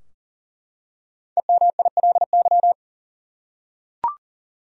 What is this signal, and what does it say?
Morse code, heard as a keyed tone, about 39 wpm, 700 Hz. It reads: EMILY
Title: USA Common Names - 40wpm